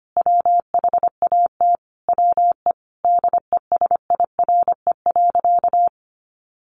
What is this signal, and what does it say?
Morse code, heard as a keyed tone, about 25 wpm, 700 Hz. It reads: WHAT WE DESIRE.
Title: Sentences from Top 1300 Words 25wpm